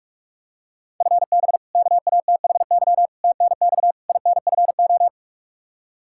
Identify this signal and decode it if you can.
Morse code, heard as a keyed tone, about 45 wpm, 700 Hz. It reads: FB KATHY TNX INFO
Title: Practice QSO - Exchange 3 - More Info - 45wpm